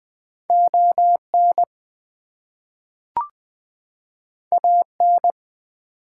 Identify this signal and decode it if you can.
Morse code, heard as a keyed tone, about 20 wpm, 700 Hz. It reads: ON AN
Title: Two Letter Words - 20wpm